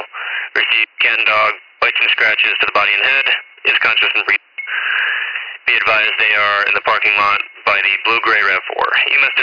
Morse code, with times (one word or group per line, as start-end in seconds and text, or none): none